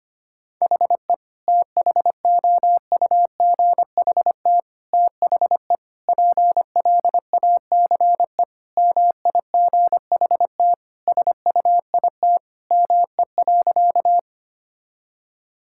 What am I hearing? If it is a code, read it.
Morse code, heard as a keyed tone, about 25 wpm, 700 Hz. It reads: HE THOUGHT THE PLACE MIGHT SUIT ME.
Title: Sentences from Top 1600 Words 25wpm